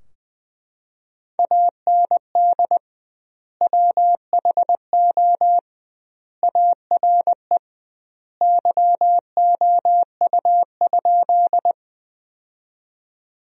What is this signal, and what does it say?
Morse code, heard as a keyed tone, about 20 wpm, 700 Hz. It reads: AND WHO ARE YOU?
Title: 2x Word-Spacing - Sentences from Top 100 Words - 20wpm